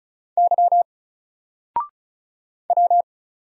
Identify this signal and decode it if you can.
Morse code, heard as a keyed tone, about 34 wpm, 700 Hz. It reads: Y W
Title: Single Letter-Number Alt 35wpm